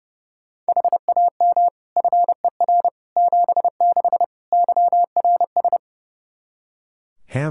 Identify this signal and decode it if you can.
Morse code, heard as a keyed tone, about 30 wpm, 700 Hz. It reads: HAM FER 76 YRS
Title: K1USN SST QSO Practice 30wpm